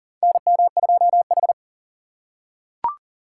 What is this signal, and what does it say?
Morse code, heard as a keyed tone, about 39 wpm, 700 Hz. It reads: NM2H
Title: USA Call Signs - 4 Characters - 2x Character Spacing - 40wpm